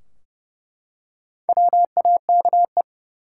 Morse code, heard as a keyed tone, about 30 wpm, 700 Hz. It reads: WAKE
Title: Four Letter Words - 30wpm